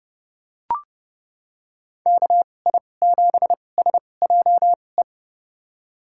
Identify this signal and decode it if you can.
Morse code, heard as a keyed tone, about 30 wpm, spaced as for 22 wpm, 700 Hz. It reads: KI7SJE
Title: USA Call Signs - 6 Characters - 2x Character Spacing - 30wpm